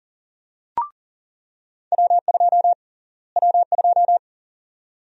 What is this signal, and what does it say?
Morse code, heard as a keyed tone, about 40 wpm, 700 Hz. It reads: W2 W2
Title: USA Call Sign Prefixes - Repeated 2x - 40wpm